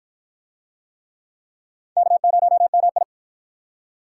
Morse code, heard as a keyed tone, about 53 wpm, 700 Hz. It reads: K0MI